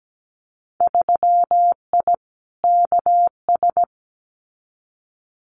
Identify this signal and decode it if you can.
Morse code, heard as a keyed tone, about 17 wpm, 700 Hz. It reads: 3I KS
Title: Winter Field Day Exchanges 17wpm